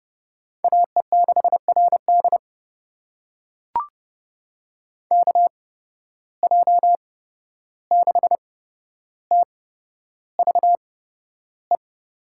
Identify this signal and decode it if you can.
Morse code, heard as a keyed tone, about 32 wpm, spaced as for 12 wpm, 700 Hz. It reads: AE6RD KJ6TVE
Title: USA Call Signs - 8x Character Spacing - 30wpm